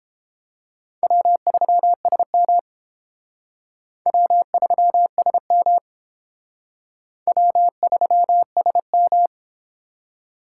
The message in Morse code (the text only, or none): W3SM W3SM W3SM